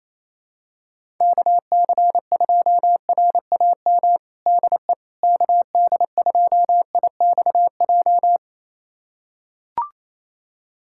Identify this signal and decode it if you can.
Morse code, heard as a keyed tone, about 28 wpm, 700 Hz. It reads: KC2RAM DE KD2IXJ
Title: USA Callsign DE Callsign - 6 Characters - 28wpm